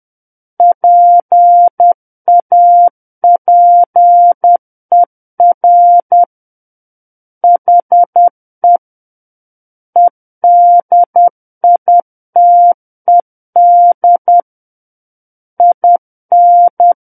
Morse code, heard as a keyed tone, about 10 wpm, 700 Hz. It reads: PAPER HE EDITED IN
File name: Book One / 01-Chapter 1 - 10 WPM